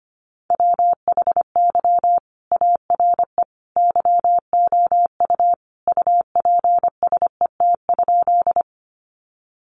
Morse code, heard as a keyed tone, about 25 wpm, 700 Hz. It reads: WHY ARE YOU UPSET?
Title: Sentences from Top 800 Words 25wpm